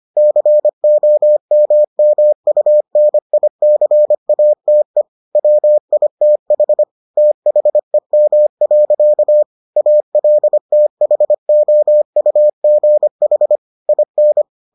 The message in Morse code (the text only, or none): COMMUNICATE WITH THEM. ALTHOUGH IN